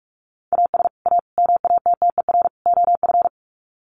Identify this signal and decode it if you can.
Morse code, heard as a keyed tone, about 44 wpm, 700 Hz. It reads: AS A MATTER OF
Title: Sets of 4 Words 45wpm